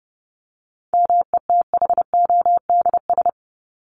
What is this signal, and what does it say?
Morse code, heard as a keyed tone, about 30 wpm, 700 Hz. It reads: METHODS
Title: ES:S Suffix - Top 100 Words with Base Words 30wpm